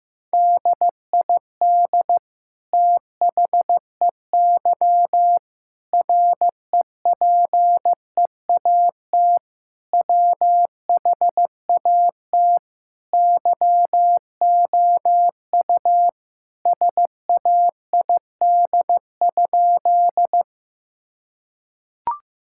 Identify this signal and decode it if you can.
Morse code, heard as a keyed tone, about 15 wpm, 700 Hz. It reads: DID THEY REPEAT WHAT YOU SAID?
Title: Sentences from Top 1600 Words 15wpm